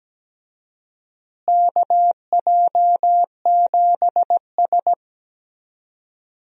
Morse code, heard as a keyed tone, about 17 wpm, 700 Hz. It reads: KJ7S
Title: USA Call Signs - 4 Characters - Repeated 2x - 17wpm